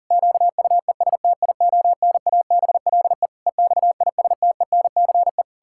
Morse code, heard as a keyed tone, about 40 wpm, 700 Hz. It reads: QUESTIONABLE EXISTENCE
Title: Pangram Challenge v2 - 40wpm